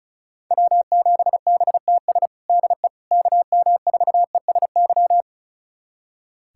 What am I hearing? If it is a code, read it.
Morse code, heard as a keyed tone, about 36 wpm, 700 Hz. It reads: W7BTS DE KM4ESY
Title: USA Callsign DE Callsign 35wpm